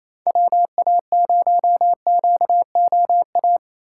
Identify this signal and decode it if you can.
Morse code, heard as a keyed tone, about 33 wpm, 700 Hz. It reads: WA0QOA
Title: Speed Racing - USA 2x3 Call Signs 28wpm